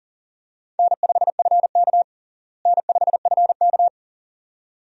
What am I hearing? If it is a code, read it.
Morse code, heard as a keyed tone, about 40 wpm, 700 Hz. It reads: N5FK N5FK
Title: USA Call Signs - 4 Characters - Repeated 2x - 40wpm